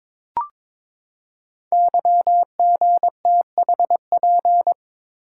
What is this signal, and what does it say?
Morse code, heard as a keyed tone, about 22 wpm, 700 Hz. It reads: YGTHP
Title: Code Groups - 5 Letters - 22wpm